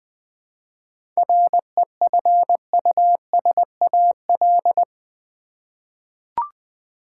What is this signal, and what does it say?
Morse code, heard as a keyed tone, about 20 wpm, 700 Hz. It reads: REFUSAL
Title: 7 Letter Words - First 3 Letters Given - 20wpm